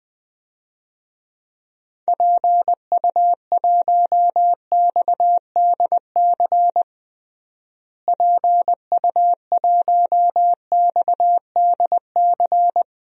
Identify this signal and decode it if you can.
Morse code, heard as a keyed tone, about 20 wpm, 700 Hz. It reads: PU1XDC PU1XDC
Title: DXCC-100 Call Signs - 6 Characters - Repeated 2x - 20wpm